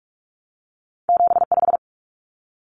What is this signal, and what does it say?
Morse code, heard as a keyed tone, about 46 wpm, 700 Hz. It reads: 75
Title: Common Cut Numbers - 2-Digit 45wpm